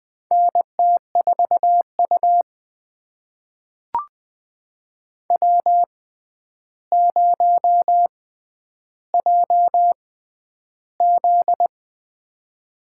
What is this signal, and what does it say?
Morse code, heard as a keyed tone, about 20 wpm, spaced as for 10 wpm, 700 Hz. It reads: NT4U W0JZ
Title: USA Call Signs - 6x Character Spacing - 20wpm